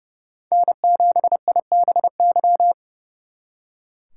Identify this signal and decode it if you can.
Morse code, heard as a keyed tone, about 36 wpm, 700 Hz. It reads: N7IBY
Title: Speed Racing - USA 1x3 Call Signs 30wpm